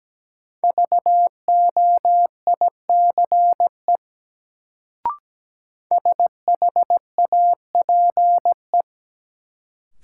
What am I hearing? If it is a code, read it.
Morse code, heard as a keyed tone, about 17 wpm, 700 Hz. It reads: VOICE SHAPE
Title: Sentences from Top 500 Words - New Word Drill - 17wpm